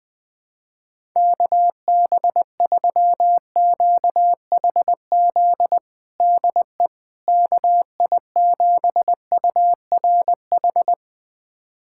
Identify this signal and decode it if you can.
Morse code, heard as a keyed tone, about 20 wpm, 700 Hz. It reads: KB3QHZ DE KI7URH
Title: USA Callsign DE Callsign - 6 Characters - 20wpm